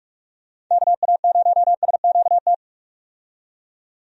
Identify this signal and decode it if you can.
Morse code, heard as a keyed tone, about 46 wpm, 700 Hz. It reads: KA0SQT
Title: USA Call Signs - 6 Characters - 45wpm